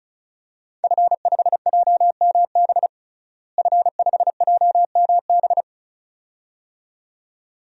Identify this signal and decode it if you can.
Morse code, heard as a keyed tone, about 34 wpm, 700 Hz. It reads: F5JMB F5JMB
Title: DXCC-100 Call Signs - 5 Characters - Repeated 2x - 35wpm